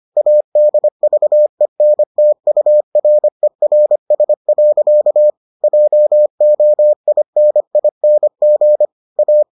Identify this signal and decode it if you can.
Morse code, heard as a keyed tone, about 25 wpm, 600 Hz. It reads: ADVENTURERS. JOINING A